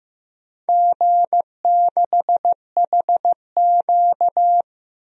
Morse code, heard as a keyed tone, about 15 wpm, 700 Hz. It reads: G6HQ
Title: DXCC-100 Call Signs - 4 Characters - Repeated 3x - 15wpm